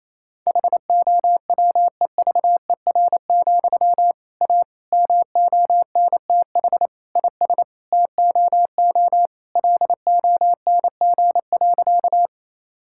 Morse code, heard as a keyed tone, about 28 wpm, 700 Hz. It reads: HOWEVER, A MONTH IS TOO LONG.
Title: Sentences from Top 1000 Words - Review Part 5 - 28wpm